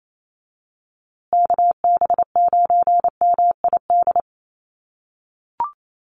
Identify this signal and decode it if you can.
Morse code, heard as a keyed tone, about 28 wpm, 700 Hz. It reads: KB9MID
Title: USA Call Signs Repeated 2x 28wpm